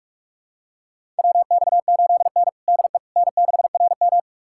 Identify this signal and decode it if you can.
Morse code, heard as a keyed tone, about 43 wpm, 700 Hz. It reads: WX8N DE N6RM
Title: USA Callsign DE Callsign - 4 Characters - 45wpm